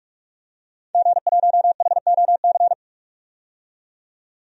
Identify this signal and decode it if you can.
Morse code, heard as a keyed tone, about 47 wpm, 700 Hz. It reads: G1HOC
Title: DXCC-100 Call Signs - 5 Characters - 45wpm